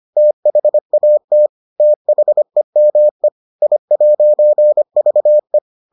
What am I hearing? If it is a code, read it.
Morse code, heard as a keyed tone, about 25 wpm, 600 Hz. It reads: THAT THEME I'VE